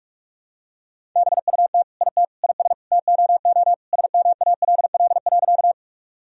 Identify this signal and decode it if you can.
Morse code, heard as a keyed tone, about 44 wpm, 700 Hz. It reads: BUT IT IS TOO SMALL.
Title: Sentences from Top 1000 Words - Review Part 1 - 45wpm